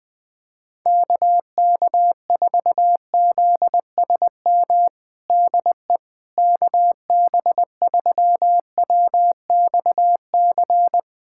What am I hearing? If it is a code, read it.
Morse code, heard as a keyed tone, about 20 wpm, 700 Hz. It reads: KK4ZSM DE KB3WXC
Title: USA Callsign DE Callsign - 6 Characters - 20wpm